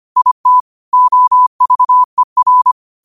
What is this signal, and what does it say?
Morse code, heard as a keyed tone, about 25 wpm, 1000 Hz. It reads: IT OVER